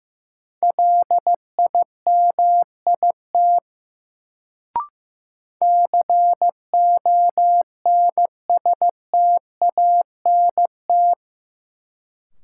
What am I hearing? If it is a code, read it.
Morse code, heard as a keyed tone, about 15 wpm, 700 Hz. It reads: LIMIT CONSTANT
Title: Sentences from Top 1500 Words - New Word Drill Alt - 15wpm